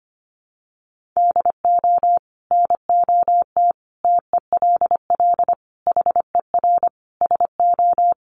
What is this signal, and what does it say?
Morse code, heard as a keyed tone, about 25 wpm, 700 Hz. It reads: DO NOT TELL HER SO
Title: Sentences from Top 1500 Words 25wpm